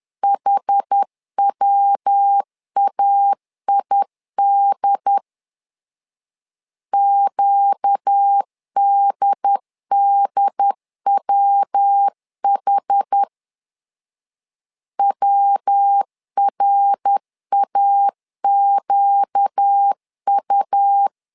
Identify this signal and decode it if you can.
Morse code, heard as a keyed tone, about 10 wpm, 800 Hz. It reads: HWAID QDDWH WRAQU
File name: LEZIONI / 4 Q 10wpm 2210